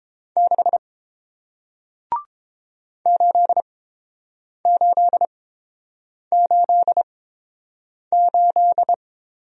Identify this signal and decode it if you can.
Morse code, heard as a keyed tone, about 30 wpm, spaced as for 7 wpm, 700 Hz. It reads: 6 8888